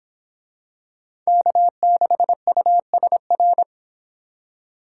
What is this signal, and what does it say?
Morse code, heard as a keyed tone, about 23 wpm, 700 Hz. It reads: K6USR